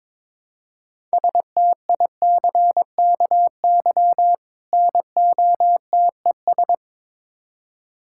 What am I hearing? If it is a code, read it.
Morse code, heard as a keyed tone, about 22 wpm, 700 Hz. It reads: STICKY NOTES
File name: Difficult Words with Hints - 22wpm